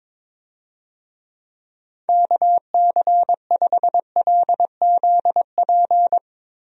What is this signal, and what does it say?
Morse code, heard as a keyed tone, about 21 wpm, 700 Hz. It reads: KC5LZP
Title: USA Call Signs - 6 Characters - Repeated 2x - 22wpm